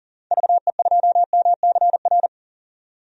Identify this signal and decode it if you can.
Morse code, heard as a keyed tone, about 39 wpm, 700 Hz. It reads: VE2MCR
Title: DXCC-100 Call Signs - 6 Characters - 40wpm